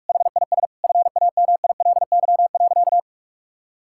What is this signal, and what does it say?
Morse code, heard as a keyed tone, about 43 wpm, 700 Hz. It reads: HIS FAMILY.
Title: Sentences from Top 1000 Words - Review Part 6 - 45wpm